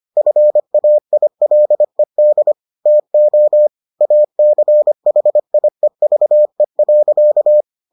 Morse code, heard as a keyed tone, about 25 wpm, 600 Hz. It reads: FAILED TO ACHIEVE.